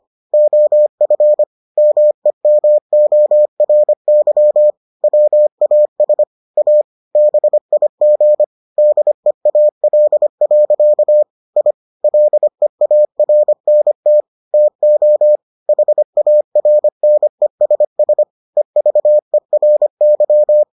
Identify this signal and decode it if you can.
Morse code, heard as a keyed tone, about 25 wpm, 600 Hz. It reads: OF MEMORY WAS A BIG DEAL. I LEARNT TO HARNESS EVERY